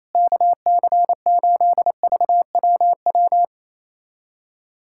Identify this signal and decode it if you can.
Morse code, heard as a keyed tone, about 27 wpm, 700 Hz. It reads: KC8VWW
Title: USA Callsign DE Callsign - 6 Characters - 28wpm